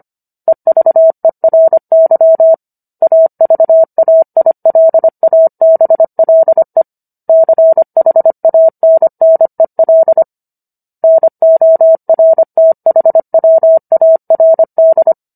Morse code, heard as a keyed tone, about 25 wpm, 650 Hz. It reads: EVERY AVAILABLE CHANNEL NORTHWARD